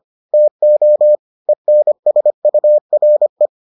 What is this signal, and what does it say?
Morse code, heard as a keyed tone, about 25 wpm, 600 Hz. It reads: TO ENSURE